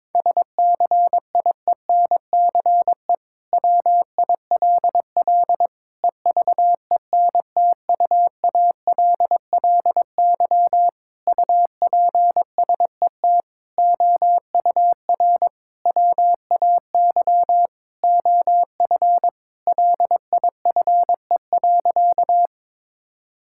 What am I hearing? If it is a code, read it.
Morse code, heard as a keyed tone, about 22 wpm, 700 Hz. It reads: SCIENCE WILL EVENTUALLY UPSET OUR WAY OF LIFE.
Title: Sentences from Top 1000 Words - Review Part 6 - 22wpm